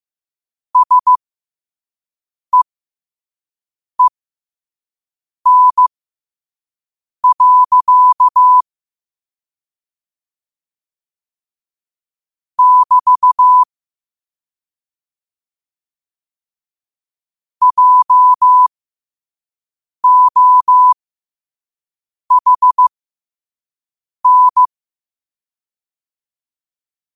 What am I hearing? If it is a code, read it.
Morse code, heard as a keyed tone, about 15 wpm, spaced as for 5 wpm, 1000 Hz. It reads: SEEN. <BT> JOHN
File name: qotdfiles / 5wpm